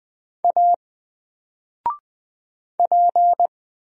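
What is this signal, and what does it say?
Morse code, heard as a keyed tone, about 20 wpm, 700 Hz. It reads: A P
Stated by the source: Single letter and number Morse drill at 20 wpm, from N4LCD.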